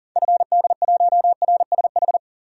Morse code, heard as a keyed tone, about 40 wpm, 700 Hz. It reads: FD1RSH
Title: DXCC-100 Call Signs - 6 Characters - 40wpm